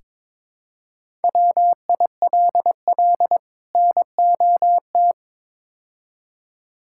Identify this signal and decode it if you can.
Morse code, heard as a keyed tone, about 22 wpm, 700 Hz. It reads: WILL NOT
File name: practice / 22wpm / Build A Phrase - 5-Word Phrases - 22wpm